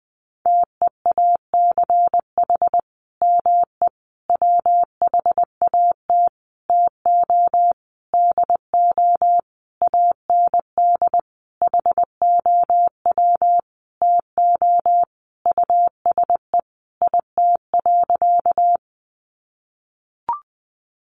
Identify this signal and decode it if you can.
Morse code, heard as a keyed tone, about 20 wpm, 700 Hz. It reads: TEACH ME WHAT TO DO AND HOW TO USE IT.
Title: Sentences from Top 500 Words 20wpm